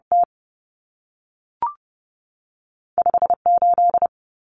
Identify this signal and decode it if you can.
Morse code, heard as a keyed tone, about 30 wpm, 700 Hz. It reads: T 5 8
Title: Common Cut Numbers - 2-Digit 30wpm